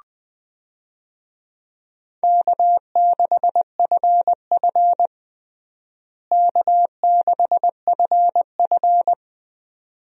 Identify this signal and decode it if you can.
Morse code, heard as a keyed tone, about 20 wpm, 700 Hz. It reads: K6FF K6FF
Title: USA Call Signs - 4 Characters - Repeated 2x - 20wpm